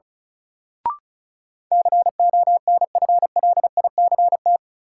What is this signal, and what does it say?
Morse code, heard as a keyed tone, about 36 wpm, 700 Hz. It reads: CONFLICT
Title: Sentences from Top 1300 Words - New Word Drill Alt - 35wpm